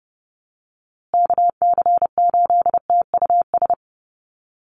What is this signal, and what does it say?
Morse code, heard as a keyed tone, about 30 wpm, 700 Hz. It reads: KC8TUS
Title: USA Call Signs Repeated 2x 30wpm